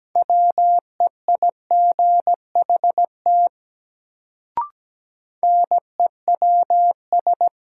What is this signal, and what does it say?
Morse code, heard as a keyed tone, about 17 wpm, 700 Hz. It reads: WEIGHT NEWS
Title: Sentences from Top 800 Words - New Word Drill - 17wpm